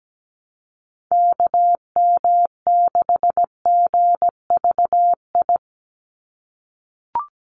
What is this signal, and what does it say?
Morse code, heard as a keyed tone, about 17 wpm, 700 Hz. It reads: KM6GVI
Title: USA Call Signs 17wpm